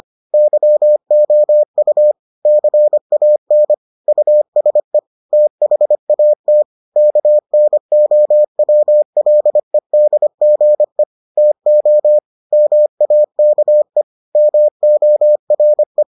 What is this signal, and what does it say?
Morse code, heard as a keyed tone, about 25 wpm, 600 Hz. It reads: YOU CAN USE THAT KNOWLEDGE TO MAKE MORE